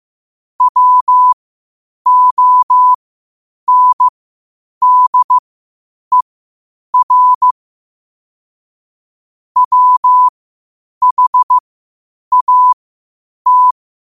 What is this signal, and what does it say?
Morse code, heard as a keyed tone, about 15 wpm, spaced as for 8 wpm, 1000 Hz. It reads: WONDER WHAT
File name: qotdfiles / 8wpm